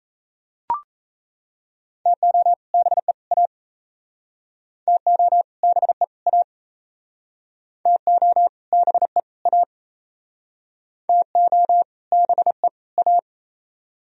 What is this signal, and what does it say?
Morse code, heard as a keyed tone, about 38 wpm, 700 Hz. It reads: TO BE A TO BE A TO BE A TO BE A